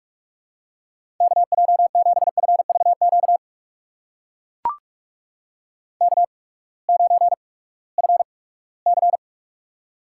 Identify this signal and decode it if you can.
Morse code, heard as a keyed tone, about 44 wpm, spaced as for 22 wpm, 700 Hz. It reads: KJ7FVQ K9FC
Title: USA Call Signs - 8x Character Spacing - 45wpm